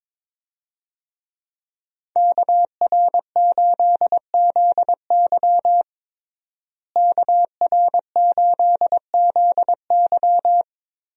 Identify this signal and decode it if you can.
Morse code, heard as a keyed tone, about 22 wpm, 700 Hz. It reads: KR8ZY KR8ZY
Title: USA Call Signs - 5 Characters - Repeated 3x - 22wpm